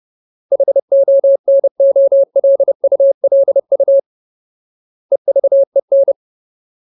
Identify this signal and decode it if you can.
Morse code, heard as a keyed tone, about 30 wpm, 550 Hz. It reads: HONOLULU EVEN
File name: Ultimate Word List / 30WPM Ultimate